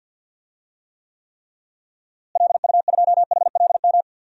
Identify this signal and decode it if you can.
Morse code, heard as a keyed tone, about 52 wpm, 700 Hz. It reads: LU2HLM
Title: DXCC-100 Call Signs - 6 Characters - Repeated 3x - 50wpm